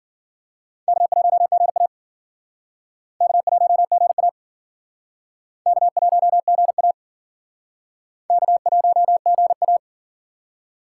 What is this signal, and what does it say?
Morse code, heard as a keyed tone, about 51 wpm, 700 Hz. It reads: K1GA K1GA K1GA K1GA